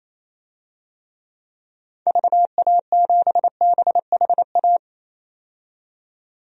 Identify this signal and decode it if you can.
Morse code, heard as a keyed tone, about 28 wpm, 700 Hz. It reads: VA7BHA
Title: DXCC-100 Call Signs - 6 Characters - 28wpm